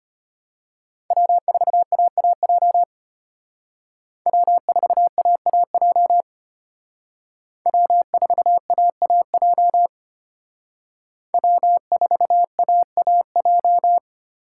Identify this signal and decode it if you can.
Morse code, heard as a keyed tone, about 32 wpm, 700 Hz. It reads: W4AAJ W4AAJ W4AAJ W4AAJ